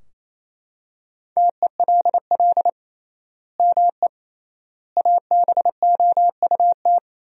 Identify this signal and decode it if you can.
Morse code, heard as a keyed tone, about 28 wpm, 700 Hz. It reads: TELL ME ABOUT
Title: 3x Word-Spacing - Sentences from Top 200 Words - 28wpm